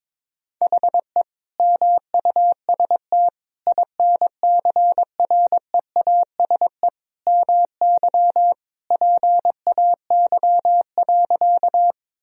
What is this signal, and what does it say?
Morse code, heard as a keyed tone, about 21 wpm, 700 Hz. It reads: HE MUST INCREASE MY PAY.
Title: Sentences from Top 1000 Words - Review Part 4 - 22wpm